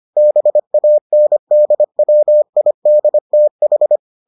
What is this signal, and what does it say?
Morse code, heard as a keyed tone, about 25 wpm, 600 Hz. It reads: BANDWIDTH